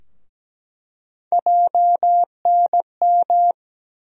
Morse code, heard as a keyed tone, about 17 wpm, 700 Hz. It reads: JNM
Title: USA Call Sign Postfixes - 3 Letters 17wpm